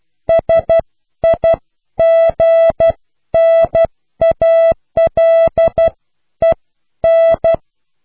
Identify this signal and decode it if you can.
Morse code, heard as a keyed tone, about 12 wpm, spaced as for 10 wpm, 660 Hz. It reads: SIGNALEN